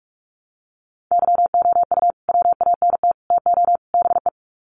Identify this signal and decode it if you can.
Morse code, heard as a keyed tone, about 44 wpm, 700 Hz. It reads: YOU WANT TO BE
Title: Sets of 4 Words 45wpm